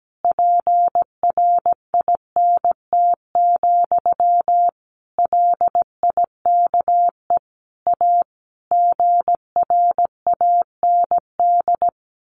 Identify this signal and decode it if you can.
Morse code, heard as a keyed tone, about 17 wpm, 700 Hz. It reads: PRINT, LIKE A GRAND
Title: A Sound Of Thunder 17wpm